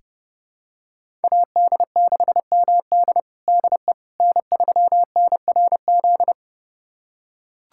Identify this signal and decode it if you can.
Morse code, heard as a keyed tone, about 30 wpm, 700 Hz. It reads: AD6MD DE N3NRZ
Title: USA Callsign DE Callsign - 5 Characters - 30wpm